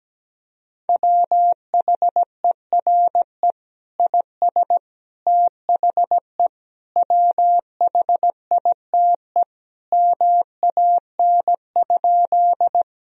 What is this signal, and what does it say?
Morse code, heard as a keyed tone, about 17 wpm, 700 Hz. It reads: WHERE IS THE WHITE MAN?
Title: Sentences from Top 1000 Words - Review Part 1 - 17wpm